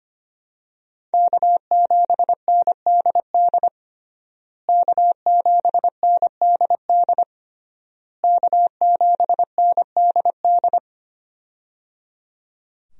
Morse code, heard as a keyed tone, about 25 wpm, 700 Hz. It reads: K7NDD K7NDD K7NDD
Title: USA Call Signs - 5 Characters - Repeated 3x - 25wpm